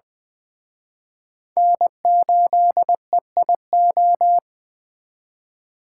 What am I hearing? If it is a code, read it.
Morse code, heard as a keyed tone, about 24 wpm, 700 Hz. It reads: N8EIO
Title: Speed Racing - USA 1x3 Call Signs 20wpm